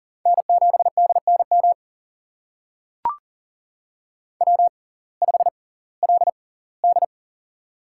Morse code, heard as a keyed tone, about 40 wpm, spaced as for 22 wpm, 700 Hz. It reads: N7DNM W5LD
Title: USA Call Signs - 6x Character Spacing - 40wpm